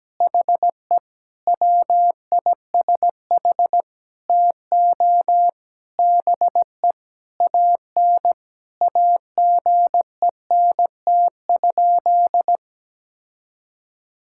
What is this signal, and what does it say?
Morse code, heard as a keyed tone, about 17 wpm, 700 Hz. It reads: HE WISH TO BE AN AGENT?
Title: Sentences from Top 1600 Words 17wpm